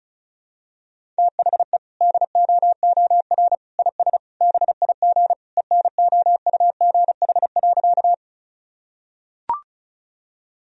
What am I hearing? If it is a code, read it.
Morse code, heard as a keyed tone, about 36 wpm, 700 Hz. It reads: THE DOOR IS BIG ENOUGH.
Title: Sentences from Top 1000 Words - Review Part 1 - 35wpm